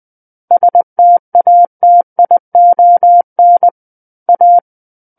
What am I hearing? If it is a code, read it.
Morse code, heard as a keyed tone, about 20 wpm, 700 Hz. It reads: STATION A
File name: 20 WPM